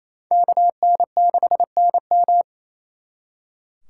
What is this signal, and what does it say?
Morse code, heard as a keyed tone, about 36 wpm, 700 Hz. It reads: KN6NM